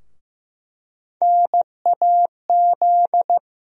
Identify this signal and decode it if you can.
Morse code, heard as a keyed tone, about 15 wpm, 700 Hz. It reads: NAZ
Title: USA Call Sign Postfixes - 3 Letters 15wpm